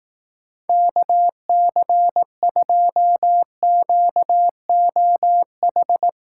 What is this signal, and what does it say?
Morse code, heard as a keyed tone, about 18 wpm, 700 Hz. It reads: KC2QOH
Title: Speed Racing - USA 2x3 Call Signs 15wpm